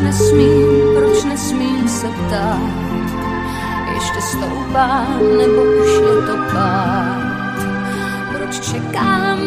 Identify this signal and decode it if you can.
Ringback tone (Europe, 425 Hz), with speech.